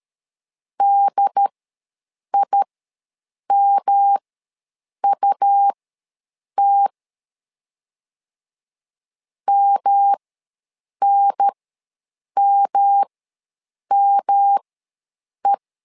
Morse code, heard as a keyed tone, about 13 wpm, spaced as for 7 wpm, 800 Hz. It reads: DIMUT MNMME
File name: LEZIONI / 3 MWRO / 3 M ws 12wpm 1745